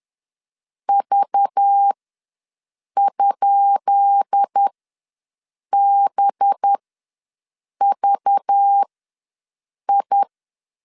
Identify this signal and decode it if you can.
Morse code, heard as a keyed tone, about 10 wpm, spaced as for 6 wpm, 800 Hz. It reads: V?BVI
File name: LEZIONI / 10 SIGNS / 10C DOT ws 10wpm 4213